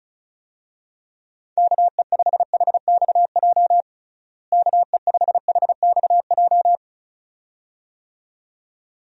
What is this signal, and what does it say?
Morse code, heard as a keyed tone, about 34 wpm, 700 Hz. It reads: KE5HXJ KE5HXJ
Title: USA Call Signs - 6 Characters - Repeated 2x - 35wpm